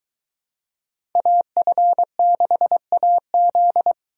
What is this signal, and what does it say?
Morse code, heard as a keyed tone, about 23 wpm, 700 Hz. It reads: AF6AZ